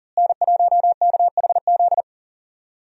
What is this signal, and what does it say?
Morse code, heard as a keyed tone, about 40 wpm, 700 Hz. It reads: N1KHZ
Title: USA Call Signs - 5 Characters - Repeated 2x - 40wpm